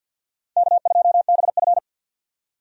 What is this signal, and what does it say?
Morse code, heard as a keyed tone, about 51 wpm, 700 Hz. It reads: K2BF
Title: USA Call Signs - 4 Characters - 50wpm